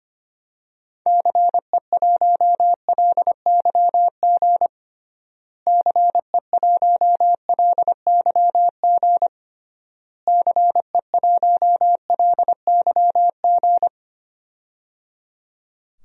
Morse code, heard as a keyed tone, about 25 wpm, 700 Hz. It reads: CE1LYG CE1LYG CE1LYG
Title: DXCC-100 Call Signs - 6 Characters - Repeated 3x - 25wpm